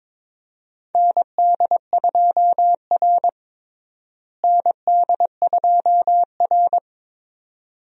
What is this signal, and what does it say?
Morse code, heard as a keyed tone, about 22 wpm, 700 Hz. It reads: ND2R ND2R
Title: USA Call Signs - 4 Characters - Repeated 2x - 22wpm